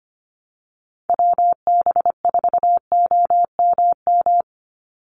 Morse code, heard as a keyed tone, about 25 wpm, 700 Hz. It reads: WB4OMM
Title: Call Signs Repeated 3x - 25wpm